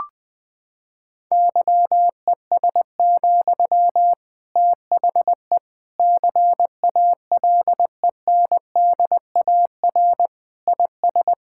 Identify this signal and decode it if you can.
Morse code, heard as a keyed tone, about 20 wpm, 700 Hz. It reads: YES, THE CALENDAR IS
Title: Sentences from Top 1600 Words 20wpm